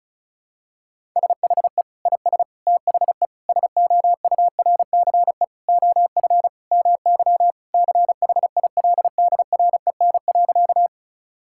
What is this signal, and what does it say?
Morse code, heard as a keyed tone, about 36 wpm, 700 Hz. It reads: SHE IS THE SOURCE OF MY CHILDREN.
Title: Sentences from Top 1000 Words - Review Part 5 - 35wpm